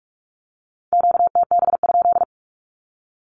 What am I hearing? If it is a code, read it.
Morse code, heard as a keyed tone, about 44 wpm, 700 Hz. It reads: QTB?
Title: Q-Codes 45wpm